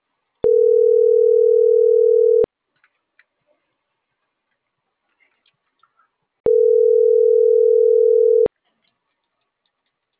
Ringback tone (North America), 440 + 480 Hz.
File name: CENTRAL NAVIERAS / out juan.salcedo